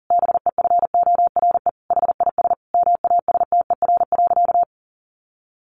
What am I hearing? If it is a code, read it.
Morse code, heard as a keyed tone, about 40 wpm, 700 Hz. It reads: BEFORE HIS MASTER.
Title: Sentences from Top 1300 Words 40wpm